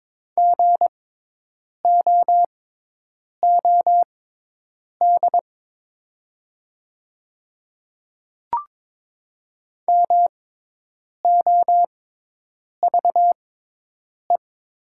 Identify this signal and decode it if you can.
Morse code, heard as a keyed tone, about 22 wpm, spaced as for 8 wpm, 700 Hz. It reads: GOOD MOVE
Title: 6x Character Spacing - Top 300 Words - 22wpm